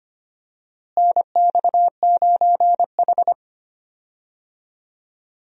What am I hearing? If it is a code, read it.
Morse code, heard as a keyed tone, about 25 wpm, 700 Hz. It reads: NX9H
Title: USA Call Signs - 4 Characters - Repeated 3x - 25wpm